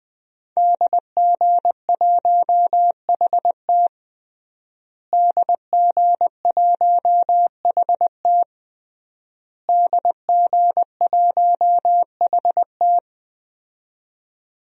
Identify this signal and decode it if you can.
Morse code, heard as a keyed tone, about 20 wpm, 700 Hz. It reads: DG1HT DG1HT DG1HT
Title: DXCC-100 Call Signs - 5 Characters - Repeated 3x - 20wpm